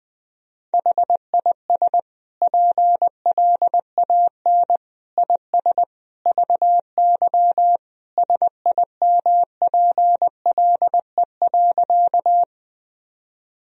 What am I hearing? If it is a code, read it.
Morse code, heard as a keyed tone, about 21 wpm, 700 Hz. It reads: HIS PLAN IS VY SIMPLE.
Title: Sentences from Top 500 Words and Abbreviations 20wpm